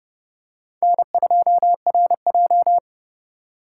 Morse code, heard as a keyed tone, about 29 wpm, 700 Hz. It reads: N2RJ